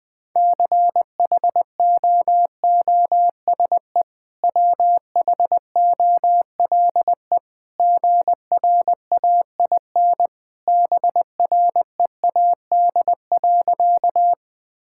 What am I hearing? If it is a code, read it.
Morse code, heard as a keyed tone, about 20 wpm, 700 Hz. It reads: CHOOSE WHOLE GRAIN BREAD.
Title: Easy Sentences with Hints - 20wpm